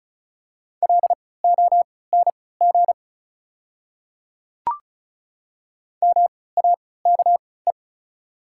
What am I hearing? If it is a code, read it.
Morse code, heard as a keyed tone, about 36 wpm, spaced as for 20 wpm, 690 Hz. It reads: LONG MAKE
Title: 3x Character Spacing - Top 100 Words - 35wpm